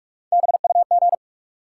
Morse code, heard as a keyed tone, about 43 wpm, 700 Hz. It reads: BUG
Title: Three Letter Words - 45wpm